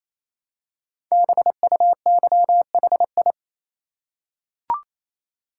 Morse code, heard as a keyed tone, about 28 wpm, 700 Hz. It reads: BUYHI
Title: Code Groups - 5 Letters - 28wpm